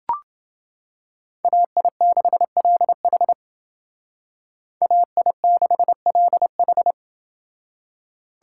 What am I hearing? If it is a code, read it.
Morse code, heard as a keyed tone, about 28 wpm, 700 Hz. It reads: AI6LH AI6LH